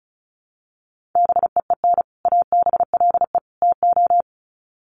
Morse code, heard as a keyed tone, about 34 wpm, 700 Hz. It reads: BEEN ABLE TO
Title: Sets of 3 Words 35wpm